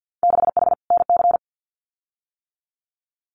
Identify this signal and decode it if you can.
Morse code, heard as a keyed tone, about 50 wpm, 700 Hz. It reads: 6H NC
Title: Winter Field Day Exchanges Encore 50wpm